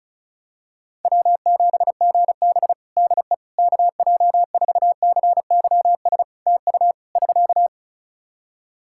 Morse code, heard as a keyed tone, about 34 wpm, 700 Hz. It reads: W7GB DE KJ4CYS TU <SK>
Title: Practice QSO - Exchange 4 - Ending - 35wpm